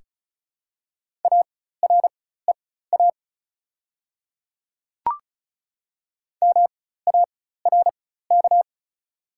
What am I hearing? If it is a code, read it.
Morse code, heard as a keyed tone, about 36 wpm, spaced as for 16 wpm, 700 Hz. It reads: AREA MARK
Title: 4x Character Spacing - Top 500 Words - 35wpm